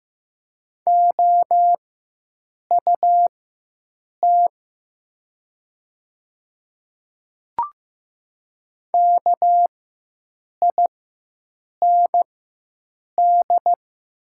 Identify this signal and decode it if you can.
Morse code, heard as a keyed tone, about 15 wpm, spaced as for 7 wpm, 700 Hz. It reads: OUT KIND
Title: 4x Character Spacing - Top 300 Words - 15wpm